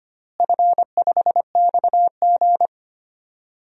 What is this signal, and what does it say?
Morse code, heard as a keyed tone, about 25 wpm, 700 Hz. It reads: F5XG
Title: DXCC-100 Call Signs - 4 Characters - Repeated 3x - 25wpm